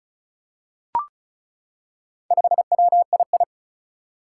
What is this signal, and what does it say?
Morse code, heard as a keyed tone, about 36 wpm, 700 Hz. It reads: 5WII
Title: Four Challenging Letter-Number Combinations 35wpm